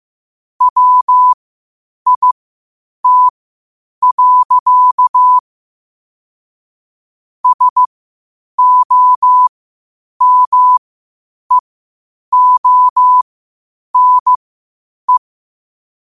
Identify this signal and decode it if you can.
Morse code, heard as a keyed tone, about 15 wpm, spaced as for 8 wpm, 1000 Hz. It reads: WIT. SOMEONE
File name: qotdfiles / 8wpm